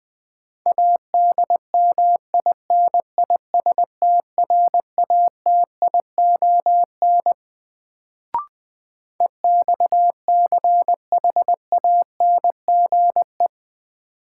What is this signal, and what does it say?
Morse code, heard as a keyed tone, about 21 wpm, 700 Hz. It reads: ADMINISTRATION EXCHANGE
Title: Sets of 5 Words - New Word Drill - 20wpm